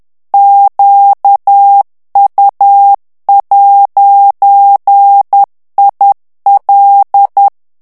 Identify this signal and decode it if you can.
Morse code, heard as a keyed tone, about 10 wpm, 800 Hz. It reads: QU'IL
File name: morse mp3 / FRANCAIS TEXTE CLAIR 2 VITESSE 10 MOTS MINUTES